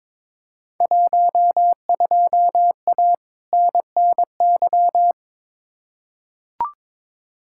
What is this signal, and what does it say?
Morse code, heard as a keyed tone, about 21 wpm, 700 Hz. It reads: 12A NNY
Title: Field Day Exchanges Encore - 22wpm